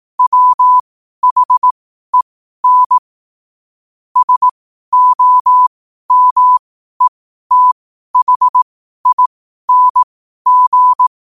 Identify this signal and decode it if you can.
Morse code, heard as a keyed tone, about 18 wpm, spaced as for 12 wpm, 1000 Hz. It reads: WHEN SOMETHING
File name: qotdfiles / 12wpm